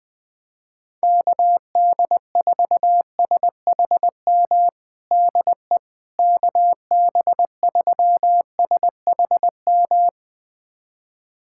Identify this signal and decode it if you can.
Morse code, heard as a keyed tone, about 20 wpm, 700 Hz. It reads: KD4SHM DE KB3SHM
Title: USA Callsign DE Callsign - 6 Characters - 20wpm